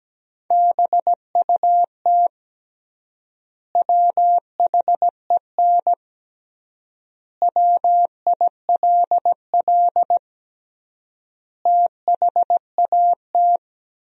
Morse code, heard as a keyed tone, about 17 wpm, 700 Hz. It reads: BUT WHEN WILL THAT
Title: 3x Word-Spacing - Sentences from Top 100 Words - 17wpm